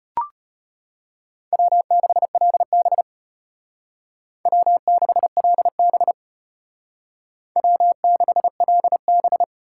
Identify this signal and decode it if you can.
Morse code, heard as a keyed tone, about 33 wpm, 700 Hz. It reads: W6LB W6LB W6LB